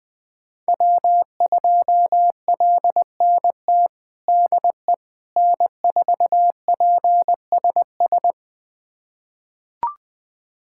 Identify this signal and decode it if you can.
Morse code, heard as a keyed tone, about 20 wpm, 700 Hz. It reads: W2LNT DE N4PSS
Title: USA Callsign DE Callsign - 5 Characters - 20wpm